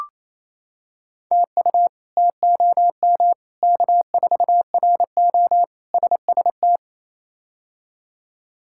Morse code, heard as a keyed tone, about 28 wpm, 700 Hz. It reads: TU TOM K4RO SST
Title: K1USN SST QSO Practice - 28wpm